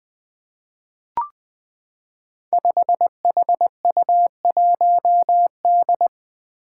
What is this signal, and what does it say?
Morse code, heard as a keyed tone, about 20 wpm, 700 Hz. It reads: 5HU1D
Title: Five Challenging Letter-Number Combinations 20wpm